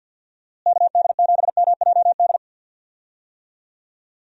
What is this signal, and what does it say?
Morse code, heard as a keyed tone, about 49 wpm, 700 Hz. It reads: KD7MJD
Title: USA Call Signs - 6 Characters - Repeated 3x - 50wpm